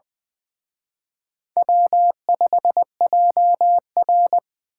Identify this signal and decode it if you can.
Morse code, heard as a keyed tone, about 24 wpm, 700 Hz. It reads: W5JR
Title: Speed Racing - USA Call Signs 20wpm